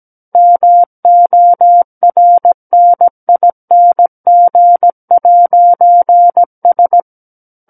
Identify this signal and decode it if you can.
Morse code, heard as a keyed tone, about 17 wpm, 700 Hz. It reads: MORNING'S
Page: SKCC Learning Center - Morse Code Practice Files